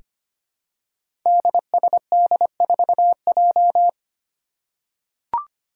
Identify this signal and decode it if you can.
Morse code, heard as a keyed tone, about 25 wpm, 700 Hz. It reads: DSD4J
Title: Five Challenging Letter-Number Combinations 25wpm